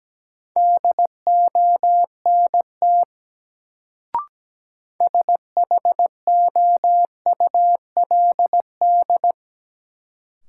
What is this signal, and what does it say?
Morse code, heard as a keyed tone, about 17 wpm, 700 Hz. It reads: DONT SHOULD
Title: Sentences from Top 300 Words - New Word Drill - 17wpm